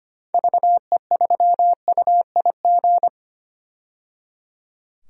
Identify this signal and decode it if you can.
Morse code, heard as a keyed tone, about 25 wpm, 700 Hz. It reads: VE3UIG
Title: DXCC-100 Call Signs - 6 Characters - 25wpm